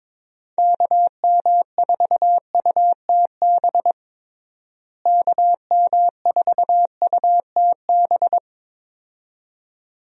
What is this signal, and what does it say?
Morse code, heard as a keyed tone, about 21 wpm, 700 Hz. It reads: KM4UTB KM4UTB
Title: USA Call Signs - 6 Characters - Repeated 2x - 22wpm